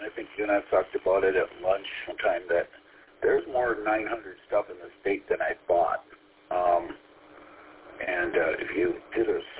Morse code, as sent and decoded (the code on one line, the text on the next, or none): none